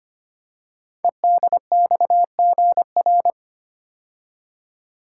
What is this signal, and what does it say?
Morse code, heard as a keyed tone, about 25 wpm, 700 Hz. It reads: EDXGR
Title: Code Groups - 5 Letters - 25wpm